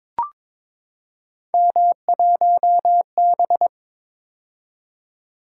Morse code, heard as a keyed tone, about 22 wpm, 700 Hz. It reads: M1B
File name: practice / 22wpm / Three Letter-Number Combinations - 22wpm